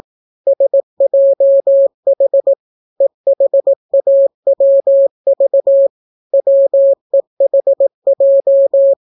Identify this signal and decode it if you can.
Morse code, heard as a keyed tone, about 18 wpm, 550 Hz. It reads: SJH EHAWV WEHJ